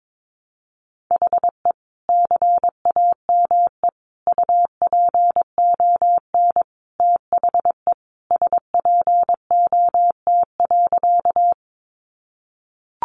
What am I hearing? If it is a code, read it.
Morse code, heard as a keyed tone, about 22 wpm, 700 Hz. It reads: HE CAME UPON THE SPOT.
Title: Sentences from Top 1100 Words 22wpm